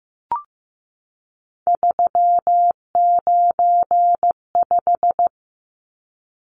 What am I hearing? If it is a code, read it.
Morse code, heard as a keyed tone, about 15 wpm, 700 Hz. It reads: 395
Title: Numbers - 3-Digit 15wpm